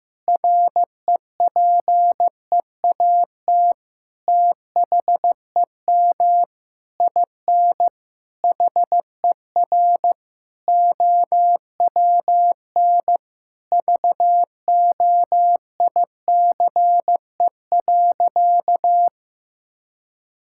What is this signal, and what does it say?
Morse code, heard as a keyed tone, about 16 wpm, 700 Hz. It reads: REPEAT THEM IN HER OWN VOICE.
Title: Sentences from Top 1600 Words 15wpm